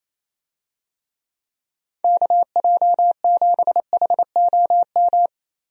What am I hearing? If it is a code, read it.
Morse code, heard as a keyed tone, about 27 wpm, 700 Hz. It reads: KJ7HOM
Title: USA Call Signs - 6 Characters - 28wpm